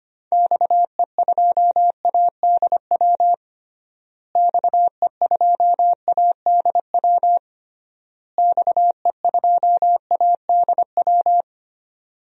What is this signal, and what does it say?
Morse code, heard as a keyed tone, about 25 wpm, 700 Hz. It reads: XE2ADW XE2ADW XE2ADW
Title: DXCC-100 Call Signs - 6 Characters - Repeated 3x - 25wpm